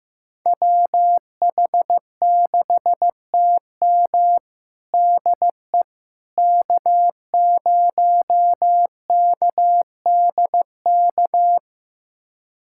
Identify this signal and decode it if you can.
Morse code, heard as a keyed tone, about 15 wpm, 700 Hz. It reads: WH6TM DE K0KDK
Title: USA Callsign DE Callsign - 5 Characters - 15wpm